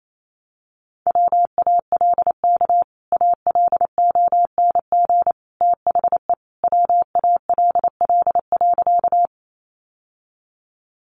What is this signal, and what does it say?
Morse code, heard as a keyed tone, about 28 wpm, 700 Hz. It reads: WALK ALONG THE WALL.
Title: Sentences from Top 1100 Words 28wpm